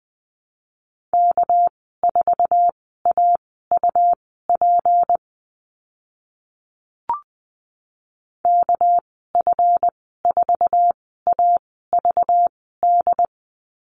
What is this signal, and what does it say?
Morse code, heard as a keyed tone, about 20 wpm, spaced as for 15 wpm, 700 Hz. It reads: K4AUP KF4AVD
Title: 219 Lesson 35g - Number 0 - Call Signs 20wpm